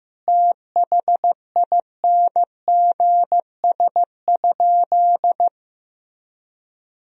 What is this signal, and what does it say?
Morse code, heard as a keyed tone, about 15 wpm, 700 Hz. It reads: THINGS?
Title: Sentences from Top 1000 Words - Review Part 4 - 15wpm